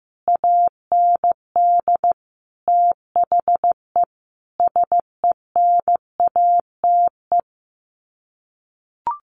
Morse code, heard as a keyed tone, about 15 wpm, 700 Hz. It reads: AND THE SENATE
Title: Sets of 5 Words Encore2 15wpm